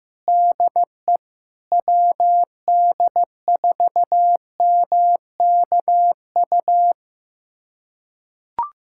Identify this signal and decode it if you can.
Morse code, heard as a keyed tone, about 15 wpm, 700 Hz. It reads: DE WD4MKU
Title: USA Callsign DE Callsign - 6 Characters - 15wpm